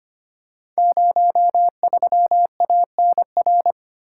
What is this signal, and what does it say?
Morse code, heard as a keyed tone, about 25 wpm, 700 Hz. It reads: 03ANR
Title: Code Groups - 5 Letter:Numbers - 25wpm